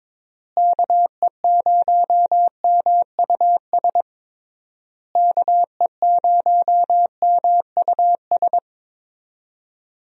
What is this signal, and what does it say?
Morse code, heard as a keyed tone, about 22 wpm, 700 Hz. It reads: KE0MUS KE0MUS
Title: USA Call Signs - 6 Characters - Repeated 3x - 22wpm